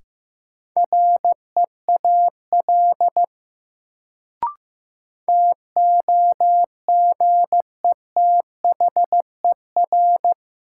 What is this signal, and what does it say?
Morse code, heard as a keyed tone, about 15 wpm, 700 Hz. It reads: REAL TOGETHER
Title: Sentences from Top 300 Words - New Word Drill - 15wpm